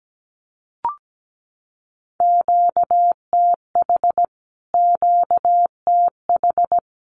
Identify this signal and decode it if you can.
Morse code, heard as a keyed tone, about 17 wpm, 700 Hz. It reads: QTH QTH
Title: Q-Codes Repeated 3x 17wpm